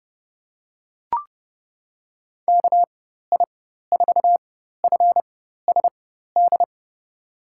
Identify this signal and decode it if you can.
Morse code, heard as a keyed tone, about 30 wpm, spaced as for 14 wpm, 700 Hz. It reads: KI4FSD
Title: USA Call Signs - 6 Characters - 4x Character Spacing - 30wpm